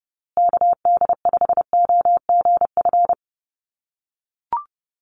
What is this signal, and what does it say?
Morse code, heard as a keyed tone, about 30 wpm, 700 Hz. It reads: KD5OGF
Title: USA Call Signs Repeated 2x 30wpm